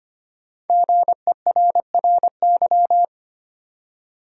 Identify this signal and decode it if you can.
Morse code, heard as a keyed tone, about 25 wpm, 700 Hz. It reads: GERRY
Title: USA Common Names - Max 5 Letters - 25wpm